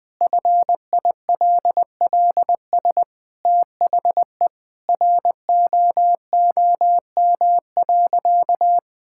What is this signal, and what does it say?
Morse code, heard as a keyed tone, about 20 wpm, 700 Hz. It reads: FILLS THE ROOM.
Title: Sentences with 2-Word Phrases - 20wpm